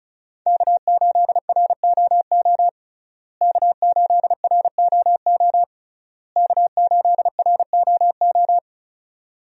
Morse code, heard as a keyed tone, about 34 wpm, 700 Hz. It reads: K8ROO K8ROO K8ROO
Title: USA Call Signs - 5 Characters - Repeated 3x - 35wpm